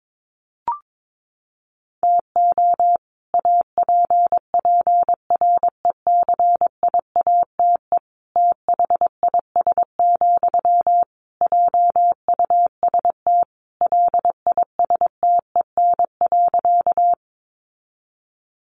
Morse code, heard as a keyed tone, about 22 wpm, 700 Hz. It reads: TO APPRECIATE THIS, JUST LISTEN.
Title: Sentences from Top 1300 Words 22wpm